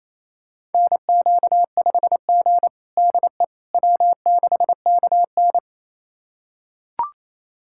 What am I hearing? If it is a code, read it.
Morse code, heard as a keyed tone, about 27 wpm, 700 Hz. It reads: NQ5G DE W6KN
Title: USA Callsign DE Callsign - 4 Characters - 28wpm